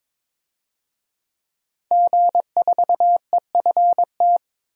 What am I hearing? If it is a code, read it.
Morse code, heard as a keyed tone, about 23 wpm, 700 Hz. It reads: G4EFT
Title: DXCC-100 Call Signs - 5 Characters - 22wpm